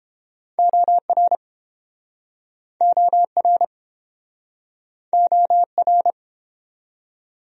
Morse code, heard as a keyed tone, about 31 wpm, 700 Hz. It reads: OR OR OR